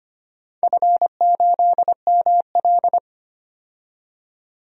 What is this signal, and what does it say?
Morse code, heard as a keyed tone, about 25 wpm, 700 Hz. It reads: F8ML
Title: DXCC-100 Call Signs - 4 Characters - 25wpm